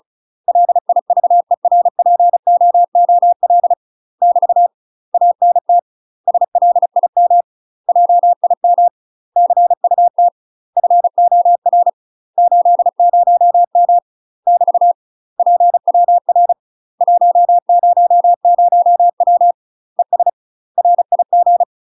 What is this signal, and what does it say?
Morse code, heard as a keyed tone, about 36 wpm, 700 Hz. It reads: LIVERPOOL <BT> ANT SLIM JIM CUT FOR 80M <BT> PWR 100W ES RIG
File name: UK MASTER QSOs TRAINING / 35WPM / UK MASTER QSOs 0023